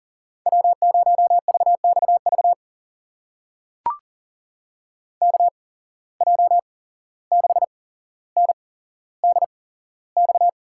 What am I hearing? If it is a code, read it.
Morse code, heard as a keyed tone, about 40 wpm, spaced as for 16 wpm, 700 Hz. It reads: W0VXV KJ6NDX
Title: USA Call Signs - 8x Character Spacing - 40wpm